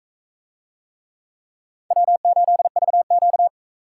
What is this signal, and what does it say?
Morse code, heard as a keyed tone, about 42 wpm, 700 Hz. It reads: W8VQ